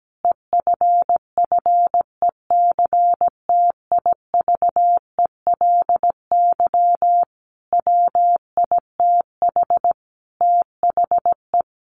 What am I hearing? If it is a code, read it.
Morse code, heard as a keyed tone, about 17 wpm, 700 Hz. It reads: EFFECTIVELY WITH THE
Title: Sentences from Top 1400 Words 17wpm